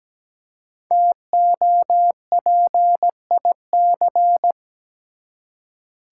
Morse code, heard as a keyed tone, about 17 wpm, 700 Hz. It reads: TOPIC
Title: Five Letter Words - 17wpm